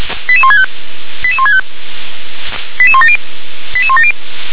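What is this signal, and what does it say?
Five-tone selective call (ZVEI), heard as quick runs of five tones, tones of about 70 ms. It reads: 88165 88165 88170 88170
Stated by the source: Five-tone ZVEI selective call 88126, received by SDR.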